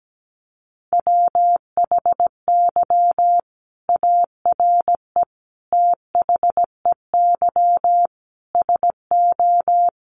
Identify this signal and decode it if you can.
Morse code, heard as a keyed tone, about 17 wpm, 700 Hz. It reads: WHY ARE THEY SO
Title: Sentences from Top 1200 Words 17wpm